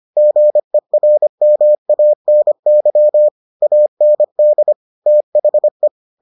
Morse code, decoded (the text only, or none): GERMANY AND THE